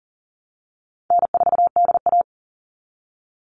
Morse code, heard as a keyed tone, about 40 wpm, 700 Hz. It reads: N4DA
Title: NA SOTA Chaser Callsigns 40wpm